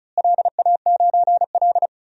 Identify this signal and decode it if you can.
Morse code, heard as a keyed tone, about 34 wpm, 700 Hz. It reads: LA9L
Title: DXCC-100 Call Signs - 4 Characters - Repeated 3x - 35wpm